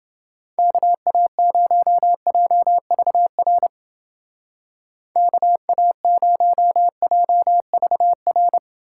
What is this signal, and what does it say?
Morse code, heard as a keyed tone, about 28 wpm, 700 Hz. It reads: KA0JVR KA0JVR